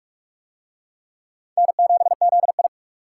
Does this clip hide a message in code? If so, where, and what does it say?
Yes: Morse code from 1.6 to 2.7 s, heard as a keyed tone, about 44 wpm, 700 Hz. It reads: N7ZI